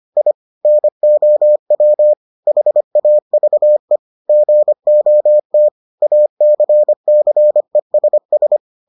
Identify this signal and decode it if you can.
Morse code, heard as a keyed tone, about 25 wpm, 600 Hz. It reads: I NOW HAVE GOT ACCESS